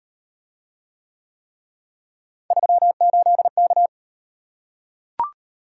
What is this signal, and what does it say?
Morse code, heard as a keyed tone, about 34 wpm, 700 Hz. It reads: 38K